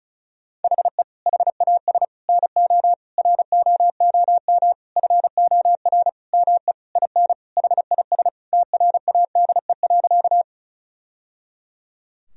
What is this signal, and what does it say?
Morse code, heard as a keyed tone, about 36 wpm, 700 Hz. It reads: HE HAS NO ROOM FOR ME IN HIS TRADE.
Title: Sentences from Top 1000 Words - Review Part 4 - 35wpm